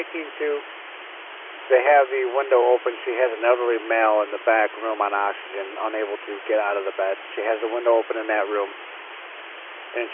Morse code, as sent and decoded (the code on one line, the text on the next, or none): none